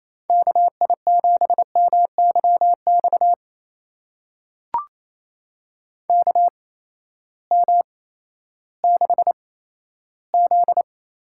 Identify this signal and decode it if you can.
Morse code, heard as a keyed tone, about 28 wpm, spaced as for 14 wpm, 700 Hz. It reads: KI7MYX KM6Z
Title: USA Call Signs - 6 Characters - 8x Character Spacing - 28wpm